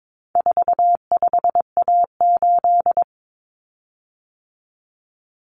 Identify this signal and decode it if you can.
Morse code, heard as a keyed tone, about 22 wpm, 700 Hz. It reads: 45A8
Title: Common Cut Numbers - 4-Digit 22wpm